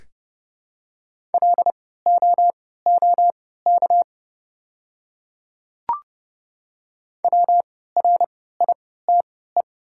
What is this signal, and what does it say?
Morse code, heard as a keyed tone, about 30 wpm, spaced as for 17 wpm, 700 Hz. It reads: LOOK WRITE
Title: 3x Character Spacing - Top 200 Words - 30wpm